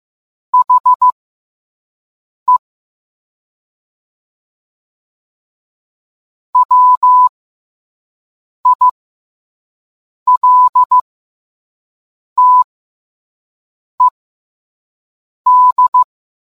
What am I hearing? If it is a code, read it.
Morse code, heard as a keyed tone, about 15 wpm, spaced as for 5 wpm, 1000 Hz. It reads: HE WILTED